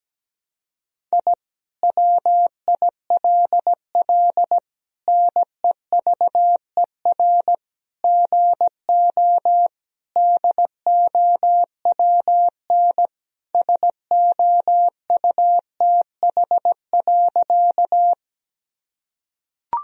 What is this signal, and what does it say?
Morse code, heard as a keyed tone, about 17 wpm, 700 Hz. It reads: I WILL NEVER GO DOWN SOUTH.
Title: Sentences from Top 1000 Words - Review Part 2 - 17wpm